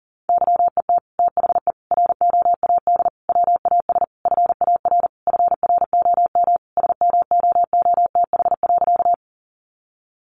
Morse code, heard as a keyed tone, about 40 wpm, 700 Hz. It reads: YET THE ROAD WAS FAR FROM SMOOTH.
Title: Sentences from Top 1400 Words 40wpm